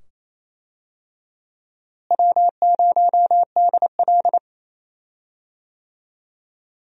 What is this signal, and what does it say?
Morse code, heard as a keyed tone, about 28 wpm, 700 Hz. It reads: W0DL
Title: USA Call Signs - 4 Characters - Repeated 2x - 28wpm